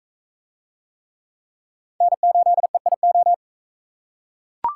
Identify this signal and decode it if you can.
Morse code, heard as a keyed tone, about 41 wpm, 700 Hz. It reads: N8EIO